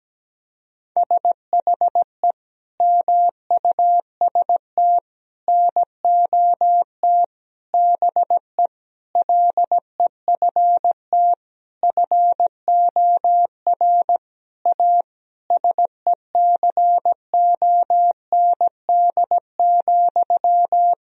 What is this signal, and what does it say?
Morse code, heard as a keyed tone, about 17 wpm, 700 Hz. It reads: SHE MUST NOT BE LEFT FOR A SECOND,
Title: Sentences from Top 1000 Words - Review Part 1 - 17wpm